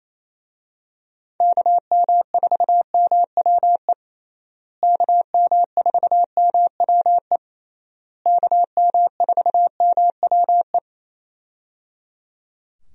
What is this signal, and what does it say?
Morse code, heard as a keyed tone, about 27 wpm, 700 Hz. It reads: KM4MWE KM4MWE KM4MWE
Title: USA Call Signs - 6 Characters - Repeated 3x - 28wpm